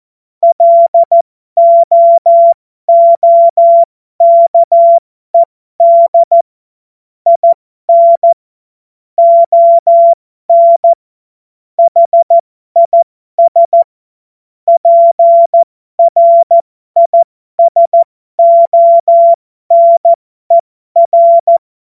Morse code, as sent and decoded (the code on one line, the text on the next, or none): .-.. --- --- -.- . -.. / .. -. / --- -. / .... .. ... / .--. .-. .. ... --- -. .-.
LOOKED IN ON HIS PRISONER